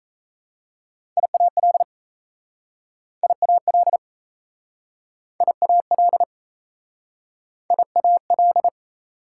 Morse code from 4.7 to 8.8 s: IAL IAL

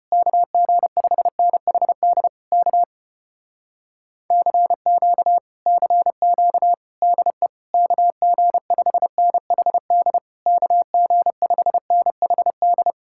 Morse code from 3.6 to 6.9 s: CQ CQ

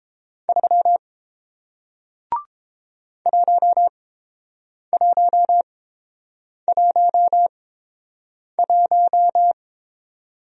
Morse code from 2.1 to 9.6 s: 1111

